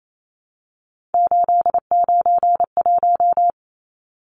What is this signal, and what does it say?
Morse code, heard as a keyed tone, about 28 wpm, 700 Hz. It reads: 891
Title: Numbers - 3-Digit 28wpm